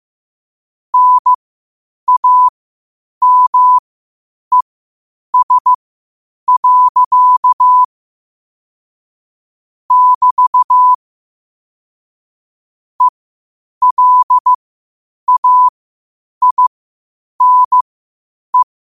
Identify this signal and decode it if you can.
Morse code, heard as a keyed tone, about 15 wpm, spaced as for 8 wpm, 1000 Hz. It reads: NAMES. <BT> ELAINE
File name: qotdfiles / 8wpm